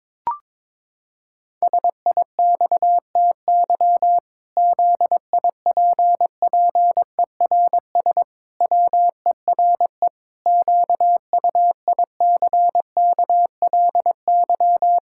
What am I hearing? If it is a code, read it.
Morse code, heard as a keyed tone, about 22 wpm, 700 Hz. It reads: SIXTY ZIPPERS WERE QUICKLY
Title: Pangram Challenge v2 - 22wpm